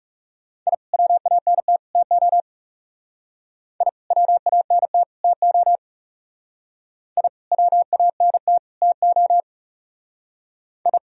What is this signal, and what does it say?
Morse code, heard as a keyed tone, about 35 wpm, 700 Hz. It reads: I WANT TO I WANT TO I WANT TO I
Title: Speed-Racing - Sets of 3 Words 30wpm